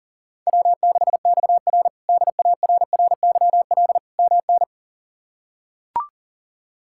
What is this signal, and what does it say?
Morse code, heard as a keyed tone, about 40 wpm, 700 Hz. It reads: W6XR DARRYL MN
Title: K1USN SST QSO Practice - 40wpm